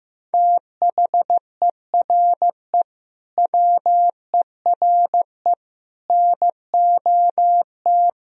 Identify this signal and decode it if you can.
Morse code, heard as a keyed tone, about 15 wpm, 700 Hz. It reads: THERE WERE NOT
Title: Sentences from Top 1000 Words - Review Part 1 - 15wpm